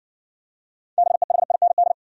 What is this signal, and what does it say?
Morse code, heard as a keyed tone, about 60 wpm, 700 Hz. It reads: BEHIND